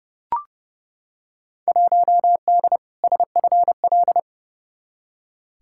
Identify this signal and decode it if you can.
Morse code, heard as a keyed tone, about 32 wpm, 700 Hz. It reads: 1D SFL
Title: Field Day Exchanges - 30wpm